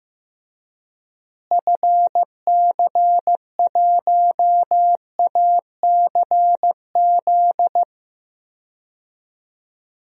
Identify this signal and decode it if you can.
Morse code, heard as a keyed tone, about 15 wpm, 700 Hz. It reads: FC1ACZ
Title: DXCC-100 Call Signs - 6 Characters - Repeated 2x - 15wpm